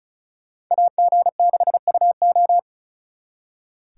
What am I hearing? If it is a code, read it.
Morse code, heard as a keyed tone, about 46 wpm, 700 Hz. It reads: AG6UO